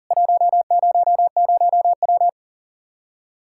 Morse code, heard as a keyed tone, about 40 wpm, 700 Hz. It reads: 100W
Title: K1USN SST QSO Practice 40wpm